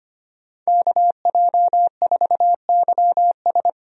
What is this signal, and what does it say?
Morse code, heard as a keyed tone, about 25 wpm, 700 Hz. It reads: KJ4YS
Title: USA Call Signs - 5 Characters - Repeated 3x - 25wpm